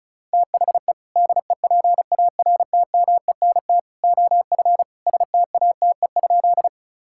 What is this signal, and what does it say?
Morse code, heard as a keyed tone, about 36 wpm, 700 Hz. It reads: THE DEPARTMENT OF STATE?
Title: Sentences from Top 1000 Words - Review Part 5 - 35wpm